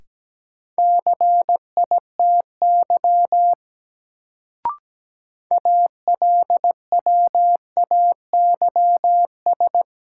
Morse code, heard as a keyed tone, about 17 wpm, 700 Hz. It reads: CITY ALWAYS
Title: Sentences from Top 300 Words - New Word Drill - 17wpm